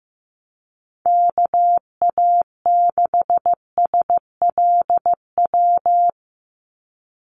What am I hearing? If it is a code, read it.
Morse code, heard as a keyed tone, about 15 wpm, 700 Hz. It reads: KA6SLW
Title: USA Call Signs 15wpm